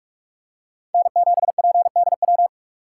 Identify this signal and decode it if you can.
Morse code, heard as a keyed tone, about 45 wpm, 700 Hz. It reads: N7PDW
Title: USA Call Signs - 5 Characters - 3x Character Spacing - 45wpm